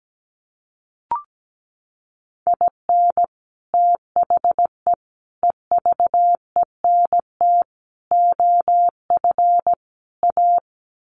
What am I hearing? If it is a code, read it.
Morse code, heard as a keyed tone, about 17 wpm, 700 Hz. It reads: IN THE EVENT OF A
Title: Sets of 5 Words Encore 17wpm